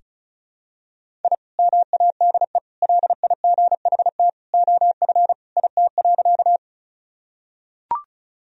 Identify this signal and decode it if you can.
Morse code, heard as a keyed tone, about 36 wpm, 700 Hz. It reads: I MADE LIGHT OF IT.
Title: Sentences from Top 1000 Words - Review Part 1 - 35wpm